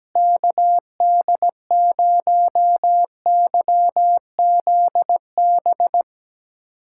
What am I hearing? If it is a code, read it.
Morse code, heard as a keyed tone, about 17 wpm, 700 Hz. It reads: KD0YZB
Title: USA Callsign DE Callsign - 6 Characters - 17wpm